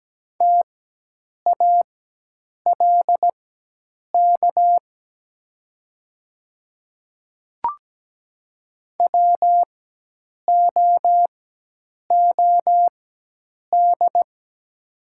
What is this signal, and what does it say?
Morse code, heard as a keyed tone, about 17 wpm, spaced as for 8 wpm, 700 Hz. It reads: TALK WOOD
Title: 4x Character Spacing - Top 500 Words - 17wpm